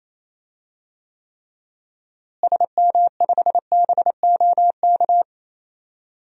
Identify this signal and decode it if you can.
Morse code, heard as a keyed tone, about 28 wpm, 700 Hz. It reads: SM5BOK
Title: DXCC-100 Call Signs - 6 Characters - Repeated 3x - 28wpm